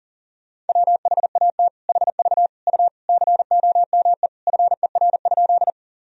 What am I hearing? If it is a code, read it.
Morse code, heard as a keyed tone, about 40 wpm, 700 Hz. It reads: WHAT HV U COME FER?
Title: Sentences from Top 100 Words and Abbreviations 40wpm